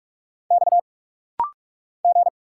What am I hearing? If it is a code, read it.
Morse code, heard as a keyed tone, about 44 wpm, 700 Hz. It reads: X G
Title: Single Letter-Number - Rapid-Fire 45wpm